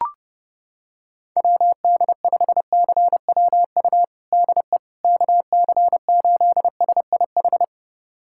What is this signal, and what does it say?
Morse code, heard as a keyed tone, about 30 wpm, 700 Hz. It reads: WD5CWU DE KC8SIH
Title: USA Callsign DE Callsign 30wpm